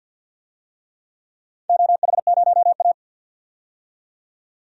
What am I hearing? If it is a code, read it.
Morse code, heard as a keyed tone, about 48 wpm, 700 Hz. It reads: OH0A
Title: DXCC-100 Call Signs - 4 Characters - 50wpm